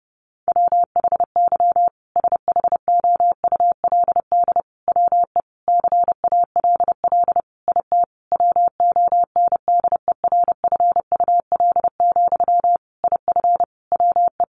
Morse code, heard as a keyed tone, about 30 wpm, 700 Hz. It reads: WHY SHOULD WE CALL IT WONDERFUL, IF WE
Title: Sentences from Top 1100 Words 30wpm